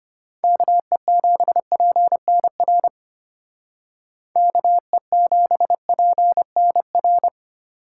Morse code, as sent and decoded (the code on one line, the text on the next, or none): -.- . --... .--. -. .-. / -.- . --... .--. -. .-.
KE7PNR KE7PNR